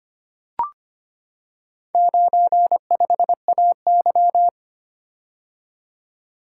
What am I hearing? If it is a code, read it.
Morse code, heard as a keyed tone, about 25 wpm, 700 Hz. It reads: 95AY
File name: practice / 25wpm / Four Letter-Number Combinations - 25wpm